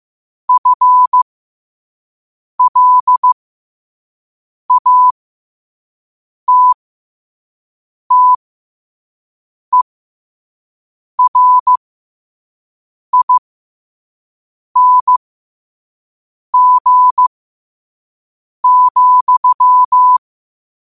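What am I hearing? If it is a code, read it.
Morse code, heard as a keyed tone, about 15 wpm, spaced as for 5 wpm, 1000 Hz. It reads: FLATTERING,
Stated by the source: Morse code sent at 5 wpm for learning.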